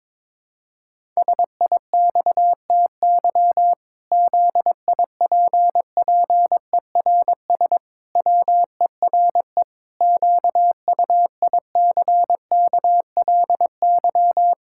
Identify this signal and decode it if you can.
Morse code, heard as a keyed tone, about 22 wpm, 700 Hz. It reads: SIXTY ZIPPERS WERE QUICKLY
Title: Pangram Challenge v2 - 22wpm